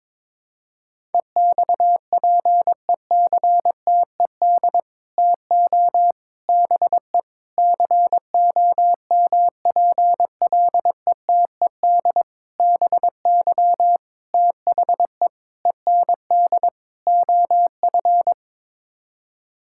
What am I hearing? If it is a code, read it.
Morse code, heard as a keyed tone, about 22 wpm, 700 Hz. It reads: EXPECTED TO BE COMPLETED BY THE END OF
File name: practice / 22wpm / Sets of 8 Words - 22wpm